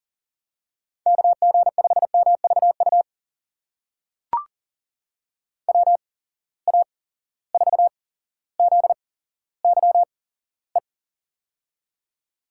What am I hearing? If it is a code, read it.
Morse code, heard as a keyed tone, about 42 wpm, spaced as for 17 wpm, 700 Hz. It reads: KG5MVU WA4ZYE
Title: USA Call Signs - 6 Characters - 8x Character Spacing - 40wpm